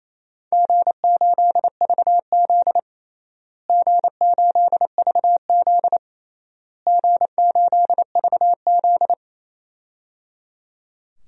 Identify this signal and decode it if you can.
Morse code, heard as a keyed tone, about 28 wpm, 700 Hz. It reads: G8VZ G8VZ G8VZ
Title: DXCC-100 Call Signs - 4 Characters - Repeated 3x - 28wpm